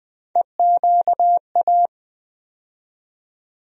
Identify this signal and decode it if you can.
Morse code, heard as a keyed tone, about 20 wpm, 700 Hz. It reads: EQA
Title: USA Call Sign Postfixes - 3 Letters 20wpm